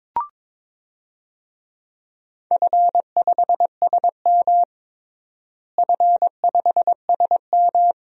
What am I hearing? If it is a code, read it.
Morse code, heard as a keyed tone, about 22 wpm, 700 Hz. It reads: F5SM F5SM
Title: DXCC-100 Call Signs - 4 Characters - Repeated 2x - 22wpm